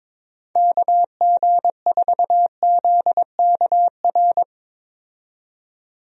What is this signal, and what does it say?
Morse code, heard as a keyed tone, about 21 wpm, 700 Hz. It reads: KG4ZKR
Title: USA Call Signs - 6 Characters - 22wpm